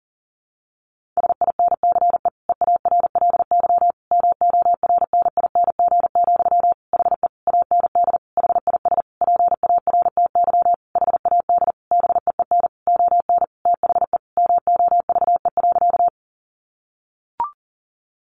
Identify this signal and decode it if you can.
Morse code, heard as a keyed tone, about 40 wpm, 700 Hz. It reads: SINCE EARLY MORNING, HE AND HIS PARTY HAD BEEN ON THE MOVE.
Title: Sentences from Top 1400 Words 40wpm